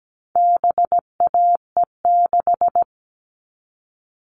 Morse code, heard as a keyed tone, about 17 wpm, 700 Hz. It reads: BAE6
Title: All Cut Numbers - 4-Digit 17wpm